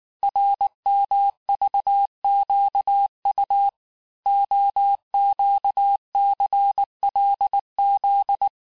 Morse code, heard as a keyed tone, about 19 wpm, 800 Hz. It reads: RMVQU OQCLZ
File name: cours-du-lundi files / cours-1589-texte-chiffre-vitesse-18mpm